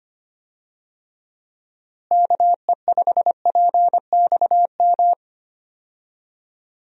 Morse code, heard as a keyed tone, about 25 wpm, 700 Hz. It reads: KE5PXM
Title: USA Call Signs - 6 Characters - Repeated 3x - 25wpm